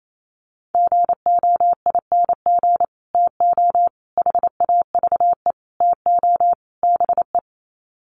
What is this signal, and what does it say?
Morse code, heard as a keyed tone, about 28 wpm, 700 Hz. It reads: GOING TO HAVE TO BE
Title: Sets of 5 Words 28wpm